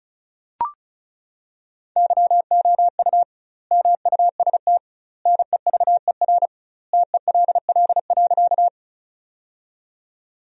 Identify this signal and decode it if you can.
Morse code, heard as a keyed tone, about 34 wpm, 700 Hz. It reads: YOU MUST NEVER TELL.
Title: 2x Word-Spacing - Sentences from Top 300 Words - 35wpm